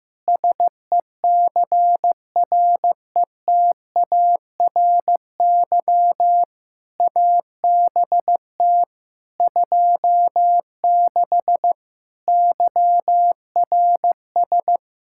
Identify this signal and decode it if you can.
Morse code, heard as a keyed tone, about 16 wpm, 700 Hz. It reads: SECRETARY ABT 26 YRS
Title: K1USN SST QSO Practice 15wpm